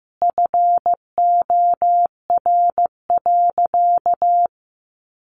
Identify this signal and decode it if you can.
Morse code, heard as a keyed tone, about 14 wpm, 700 Hz. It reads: FOR.
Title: Sentences from Top 1500 Words 15wpm